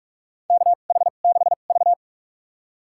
Morse code, heard as a keyed tone, about 46 wpm, spaced as for 32 wpm, 690 Hz. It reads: KH6V
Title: USA Call Signs - 4 Characters - 2x Character Spacing - 45wpm